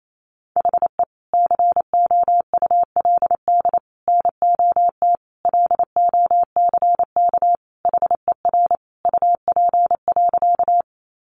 Morse code, heard as a keyed tone, about 28 wpm, 700 Hz. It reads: HE COULD NOT LOCK HER UP.
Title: Sentences from Top 1300 Words 28wpm